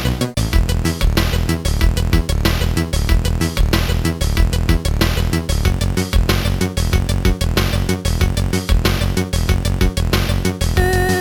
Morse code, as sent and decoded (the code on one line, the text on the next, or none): none